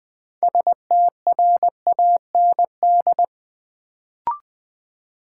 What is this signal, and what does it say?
Morse code, heard as a keyed tone, about 20 wpm, 700 Hz. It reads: STRAND
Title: Build A Word - 2 Letters At A Time - 6-Letter Words - 20wpm